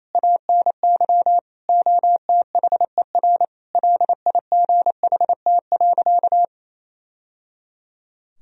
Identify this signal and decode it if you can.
Morse code, heard as a keyed tone, about 28 wpm, 700 Hz. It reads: ANY OTHER LIGHT.
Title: Sentences from Top 1000 Words - Review Part 1 - 28wpm